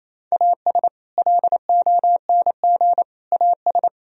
Morse code, heard as a keyed tone, about 28 wpm, 700 Hz. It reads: AS LONG AS
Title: Sentences from Top 1000 Words - Review Part 2 - 28wpm